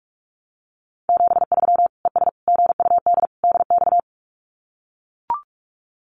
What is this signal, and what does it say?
Morse code, heard as a keyed tone, about 44 wpm, 700 Hz. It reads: 73 ES GUD DX
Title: Common QSO Phrases 45wpm